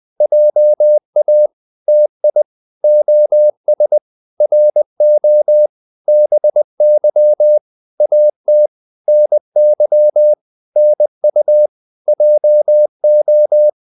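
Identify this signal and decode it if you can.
Morse code, heard as a keyed tone, about 20 wpm, 600 Hz. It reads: JA TI OS RO BY AT NY NU JO
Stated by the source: Morse code practice of short Danish words at 20 wpm, from CW med Gnister.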